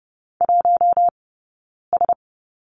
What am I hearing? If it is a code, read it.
Morse code, heard as a keyed tone, about 30 wpm, 700 Hz. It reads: 1 S
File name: Lesson 20d - Number 3 - Two Characters 30wpm